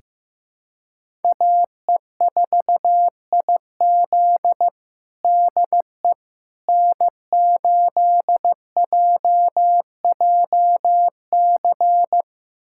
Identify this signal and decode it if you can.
Morse code, heard as a keyed tone, about 15 wpm, 700 Hz. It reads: AE4IZ DE N8JJC
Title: USA Callsign DE Callsign - 5 Characters - 15wpm